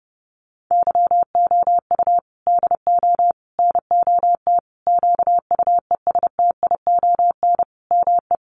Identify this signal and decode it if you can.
Morse code, heard as a keyed tone, about 30 wpm, 700 Hz. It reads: YOU DO NOT QUESTION ME
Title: Sentences from Top 1100 Words 30wpm